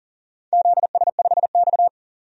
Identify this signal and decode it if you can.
Morse code, heard as a keyed tone, about 40 wpm, 690 Hz. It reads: ZS5X
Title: DXCC-100 Call Signs - 4 Characters - 40wpm